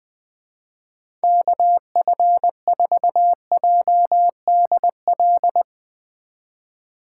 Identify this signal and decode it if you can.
Morse code, heard as a keyed tone, about 20 wpm, 700 Hz. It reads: KF4JDL
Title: Speed Racing - USA 2x3 Call Signs 17wpm